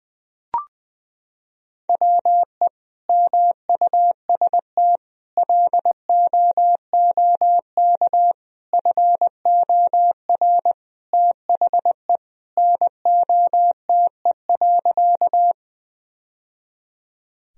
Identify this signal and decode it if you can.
Morse code, heard as a keyed tone, about 21 wpm, 700 Hz. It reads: WE MUST LOOK FOR THE NOTE.
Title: Sentences from Top 1000 Words - Review Part 2 - 20wpm